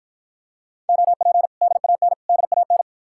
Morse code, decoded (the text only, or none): OP DAN DAN